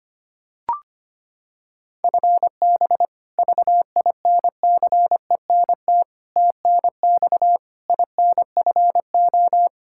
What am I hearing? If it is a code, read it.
Morse code, heard as a keyed tone, about 25 wpm, 700 Hz. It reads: FB VINCENT TNX INFO
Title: Practice QSO - Exchange 3 - More Info - 25wpm